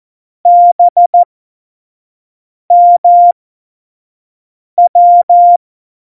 Morse code, heard as a keyed tone, about 14 wpm, spaced as for 5 wpm, 700 Hz. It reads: BMW
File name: Morsekurs / Witze / Witz 02. April 2019, 25 F70